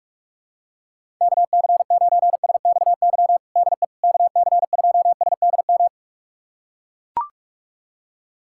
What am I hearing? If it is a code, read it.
Morse code, heard as a keyed tone, about 41 wpm, 700 Hz. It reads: KC9SXY DE KC2SDM
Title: USA Callsign DE Callsign - 6 Characters - 45wpm